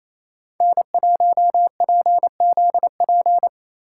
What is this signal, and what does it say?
Morse code, heard as a keyed tone, about 28 wpm, 700 Hz. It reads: N1PZP
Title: USA Call Signs - 5 Characters - 4x Character Spacing - 28wpm